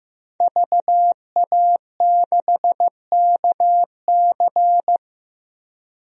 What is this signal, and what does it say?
Morse code, heard as a keyed tone, about 16 wpm, 700 Hz. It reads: VA6KC
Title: DXCC-100 Call Signs - 5 Characters - 15wpm